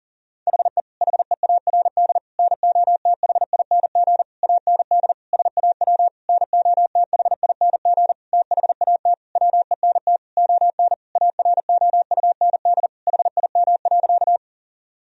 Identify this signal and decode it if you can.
Morse code, heard as a keyed tone, about 37 wpm, 700 Hz. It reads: HE HEARD NOTHING AND SAW NOTHING THAT WENT ON AROUND HIM.
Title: Sentences from Top 1000 Words - Review Part 3 - 40wpm